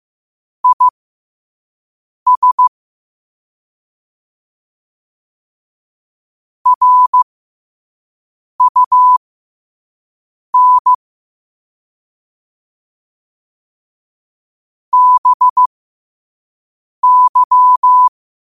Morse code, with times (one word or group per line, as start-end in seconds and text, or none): 0.6-2.7 IS
6.7-10.9 RUN
14.9-18.1 BY